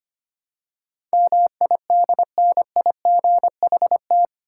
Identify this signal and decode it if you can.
Morse code, heard as a keyed tone, about 25 wpm, 700 Hz. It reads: MIDNIGHT
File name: practice / 25wpm / 8 Letter Words - First 3 Letters Given - 25wpm